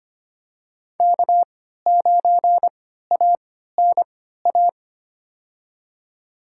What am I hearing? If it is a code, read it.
Morse code, heard as a keyed tone, about 25 wpm, spaced as for 14 wpm, 700 Hz. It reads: K9ANA
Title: USA Call Signs - 5 Characters - 3x Character Spacing - 25wpm